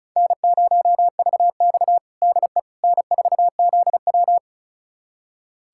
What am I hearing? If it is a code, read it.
Morse code, heard as a keyed tone, about 36 wpm, 700 Hz. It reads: N0VX DE N4ZW
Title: USA Callsign DE Callsign - 4 Characters - 35wpm